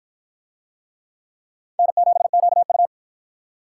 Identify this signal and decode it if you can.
Morse code, heard as a keyed tone, about 53 wpm, 700 Hz. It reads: N7QU